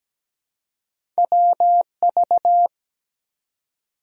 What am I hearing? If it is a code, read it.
Morse code, heard as a keyed tone, about 17 wpm, 700 Hz. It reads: WV
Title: US States and Canadian Provinces 17wpm